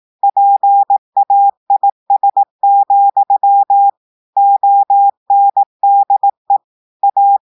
Morse code, heard as a keyed tone, about 18 wpm, 800 Hz. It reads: PAIS, ONDE A
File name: audios / portugais / portugais texte en clair 1 vitesse 18 mots minutes